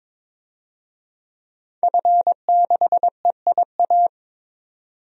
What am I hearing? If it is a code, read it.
Morse code, heard as a keyed tone, about 22 wpm, 700 Hz. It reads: F6EIA
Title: DXCC-100 Call Signs - 5 Characters - 22wpm